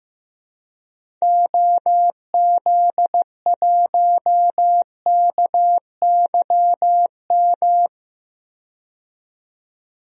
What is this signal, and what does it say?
Morse code, heard as a keyed tone, about 15 wpm, 700 Hz. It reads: OZ1KYM
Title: DXCC-100 Call Signs - 6 Characters - 15wpm